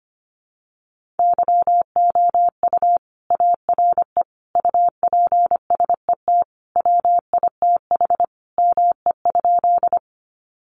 Morse code, heard as a keyed tone, about 25 wpm, 700 Hz. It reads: YOU ARE UPSET WITH ME?
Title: Sentences from Top 800 Words 25wpm